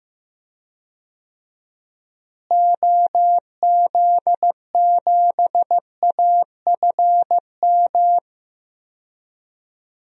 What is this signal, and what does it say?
Morse code, heard as a keyed tone, about 15 wpm, 700 Hz. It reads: OZ7AFM
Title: DXCC-100 Call Signs - 6 Characters - Repeated 3x - 15wpm